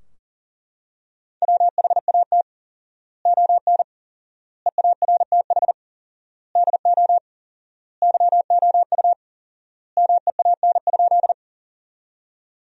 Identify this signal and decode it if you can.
Morse code, heard as a keyed tone, about 40 wpm, 700 Hz. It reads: WHAT ON EARTH DO YOU MEAN?
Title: 4x Word-Spacing - Sentences from Top 200 Words - 40wpm